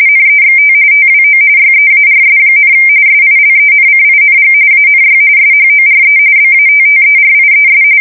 RTTY, about 45.45 baud. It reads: NEXT TWO YEARS, SILENCING ANY SHORE BATTERIES
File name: fsk170 / USS-New-Jersey